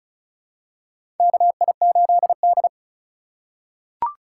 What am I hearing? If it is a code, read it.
Morse code, heard as a keyed tone, about 36 wpm, 700 Hz. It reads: KI8D
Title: USA Call Signs - 4 Characters - 4x Character Spacing - 35wpm